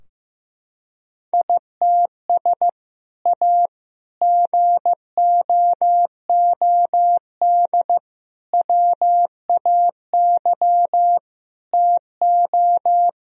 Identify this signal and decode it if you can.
Morse code, heard as a keyed tone, about 16 wpm, 700 Hz. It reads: ITS A GOOD WAY TO
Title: Sentences from Top 1000 Words - Review Part 3 - 15wpm